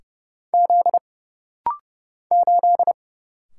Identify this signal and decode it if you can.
Morse code, heard as a keyed tone, about 30 wpm, 700 Hz. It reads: Z 8
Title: Single Letter-Number - Rapid-Fire 30wpm